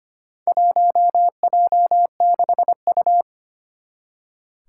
Morse code, heard as a keyed tone, about 25 wpm, 700 Hz. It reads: 1J6U
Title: Four Challenging Letter-Number Combinations 25wpm